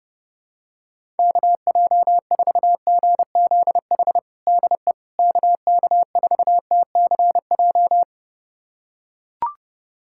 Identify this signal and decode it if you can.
Morse code, heard as a keyed tone, about 28 wpm, 700 Hz. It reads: KJ4GZH DE KK4TCJ
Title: USA Callsign DE Callsign - 6 Characters - 30wpm